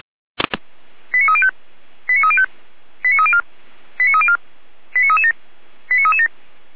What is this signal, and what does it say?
Five-tone selective call (ZVEI), heard as quick runs of five tones, tones of about 70 ms. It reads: 89305 89305 89304 89304 89337 89337
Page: ZVEI